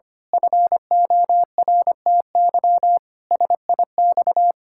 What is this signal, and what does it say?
Morse code, heard as a keyed tone, about 25 wpm, 700 Hz. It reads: FORTY SIX
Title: Pangram Challenge v2 - 25wpm